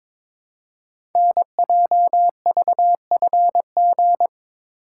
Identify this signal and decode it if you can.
Morse code, heard as a keyed tone, about 22 wpm, 700 Hz. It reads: NJVFG
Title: Code Groups - 5 Letters - 22wpm